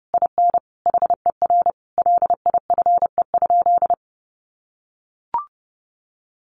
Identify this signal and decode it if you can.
Morse code, heard as a keyed tone, about 30 wpm, 700 Hz. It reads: IN HER LIFE?
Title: Sentences from Top 1500 Words 30wpm